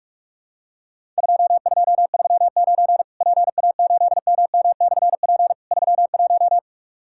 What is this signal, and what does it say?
Morse code, heard as a keyed tone, about 44 wpm, 700 Hz. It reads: 2239 PA8MM/P 31
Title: Worked All Europe DX Contest - QTC Exchange - 45wpm